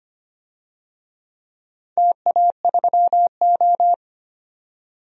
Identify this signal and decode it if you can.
Morse code, heard as a keyed tone, about 25 wpm, 700 Hz. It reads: TA3O
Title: DXCC-100 Call Signs - 4 Characters - 25wpm